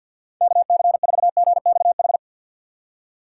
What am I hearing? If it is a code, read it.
Morse code, heard as a keyed tone, about 61 wpm, 700 Hz. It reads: KC4GXH
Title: Speed Racing - USA 2x3 Call Signs 50wpm